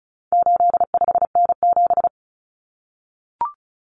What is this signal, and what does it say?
Morse code, heard as a keyed tone, about 34 wpm, 700 Hz. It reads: 85N7
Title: Common Cut Numbers - 4-Digit 35wpm